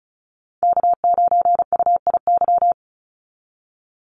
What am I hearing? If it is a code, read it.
Morse code, heard as a keyed tone, about 34 wpm, 700 Hz. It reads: K9UIY
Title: Call Signs Repeated 3x - 35wpm